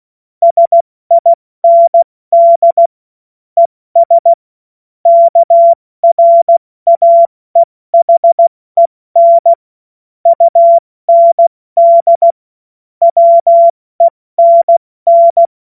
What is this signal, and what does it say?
Morse code, heard as a keyed tone, about 16 wpm, 680 Hz. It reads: SIND ES KRAEHEN UND WENN
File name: Morsekurs / Witze / Witz 20 Februar 2024 70 F80